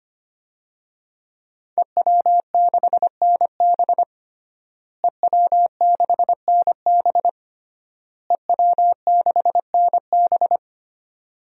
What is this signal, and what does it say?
Morse code, heard as a keyed tone, about 25 wpm, 700 Hz. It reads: EW6NB EW6NB EW6NB
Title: DXCC-100 Call Signs - 5 Characters - Repeated 3x - 25wpm